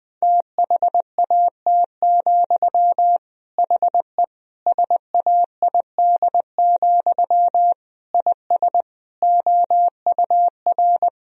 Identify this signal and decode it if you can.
Morse code, heard as a keyed tone, about 20 wpm, 700 Hz. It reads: THAT, HE SAID, IS OUR
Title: Sentences from Top 1000 Words - Review Part 3 - 20wpm